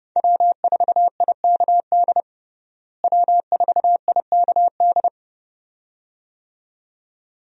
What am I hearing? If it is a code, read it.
Morse code, heard as a keyed tone, about 32 wpm, 700 Hz. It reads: W4IKD W4IKD
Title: USA Call Signs - 5 Characters - Repeated 2x - 30wpm